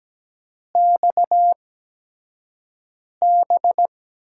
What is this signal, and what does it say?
Morse code, heard as a keyed tone, about 17 wpm, spaced as for 5 wpm, 700 Hz. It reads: XB
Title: USA Call Signs - 6 Characters - 8x Character Spacing - 17wpm